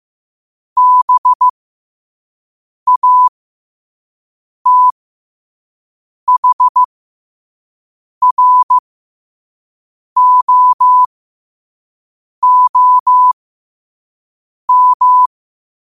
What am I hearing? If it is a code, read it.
Morse code, heard as a keyed tone, about 15 wpm, spaced as for 5 wpm, 1000 Hz. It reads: BATHROOM
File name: qotdfiles / 5wpm